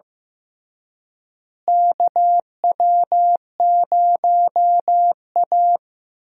Morse code, heard as a keyed tone, about 15 wpm, 700 Hz. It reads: KW0A
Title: USA Call Signs - 4 Characters - Repeated 3x - 15wpm